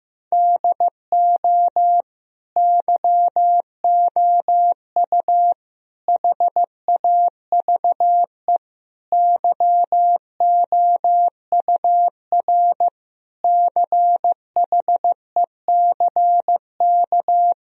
Morse code, heard as a keyed tone, about 15 wpm, 700 Hz. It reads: DO YOU HAVE YOUR CHECK
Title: Sentences from Top 1000 Words - Review Part 3 - 15wpm